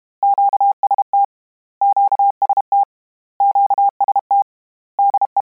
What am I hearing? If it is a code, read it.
Morse code, heard as a keyed tone, about 32 wpm, 800 Hz. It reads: QST QST QST DE
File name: Combinations / Combinations-mp3s / Combinations@30